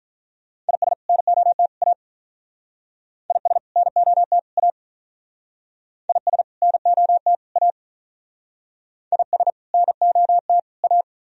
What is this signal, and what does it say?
Morse code, heard as a keyed tone, about 46 wpm, 700 Hz. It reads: IS NOT A IS NOT A IS NOT A IS NOT A